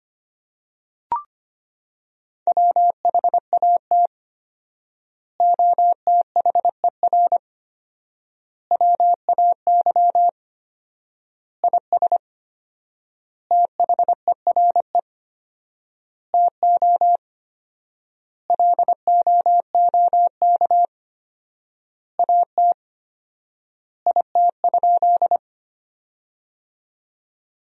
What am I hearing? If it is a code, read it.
Morse code, heard as a keyed tone, about 25 wpm, 700 Hz. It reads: WHAT OTHER WAY IS THERE TO LOOK AT IT?
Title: 4x Word-Spacing - Sentences from Top 100 Words - 25wpm